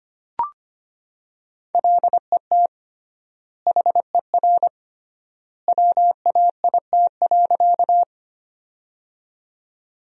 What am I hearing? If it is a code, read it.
Morse code, heard as a keyed tone, about 25 wpm, 700 Hz. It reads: LET HER WAIT.
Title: 3x Word-Spacing - Sentences from Top 500 Words - 25wpm